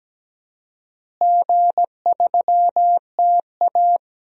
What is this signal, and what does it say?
Morse code, heard as a keyed tone, about 17 wpm, 700 Hz. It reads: G3TA
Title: DXCC-100 Call Signs - 4 Characters - Repeated 3x - 17wpm